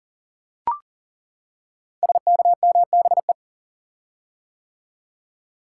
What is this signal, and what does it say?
Morse code, heard as a keyed tone, about 38 wpm, 700 Hz. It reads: SKMBE
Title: Code Groups - 5 Letters - 40wpm